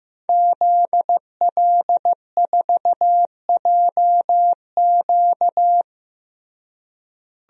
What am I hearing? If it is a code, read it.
Morse code, heard as a keyed tone, about 15 wpm, 700 Hz. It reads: ZL4JQ
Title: DXCC-100 Call Signs - 5 Characters - Repeated 3x - 15wpm